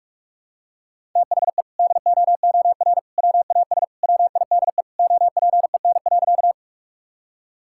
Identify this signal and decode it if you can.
Morse code, heard as a keyed tone, about 43 wpm, 700 Hz. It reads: THE DOOR WAS WIDE OPEN.
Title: Sentences from Top 1000 Words - Review Part 6 - 45wpm